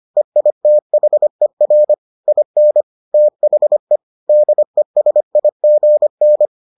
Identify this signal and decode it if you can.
Morse code, heard as a keyed tone, about 25 wpm, 600 Hz. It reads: EITHER IN THE DESIGN